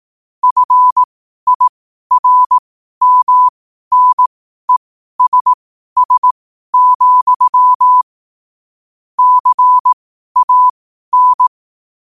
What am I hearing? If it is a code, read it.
Morse code, heard as a keyed tone, about 18 wpm, spaced as for 12 wpm, 1000 Hz. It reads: FIRMNESS, CAN
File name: qotdfiles / 12wpm